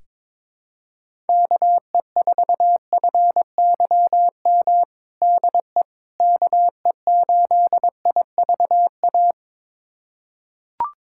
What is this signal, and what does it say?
Morse code, heard as a keyed tone, about 22 wpm, 700 Hz. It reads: KE4FYM DE KE8IVA
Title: USA Callsign DE Callsign - 6 Characters - 22wpm